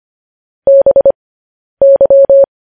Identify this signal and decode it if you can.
Morse code, heard as a keyed tone, about 25 wpm, spaced as for 10 wpm, 550 Hz. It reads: BY
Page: news-headlines-in-morse-code-at-10-wpm